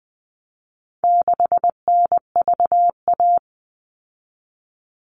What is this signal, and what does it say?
Morse code, heard as a keyed tone, about 20 wpm, 700 Hz. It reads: 6NVA
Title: All Cut Numbers - 4-Digit 20wpm